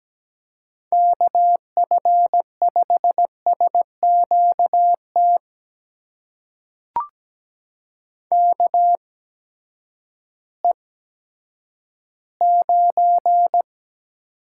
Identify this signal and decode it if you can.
Morse code, heard as a keyed tone, about 17 wpm, spaced as for 10 wpm, 700 Hz. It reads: KF5SQT KE9
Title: USA Call Signs - 8x Character Spacing - 17wpm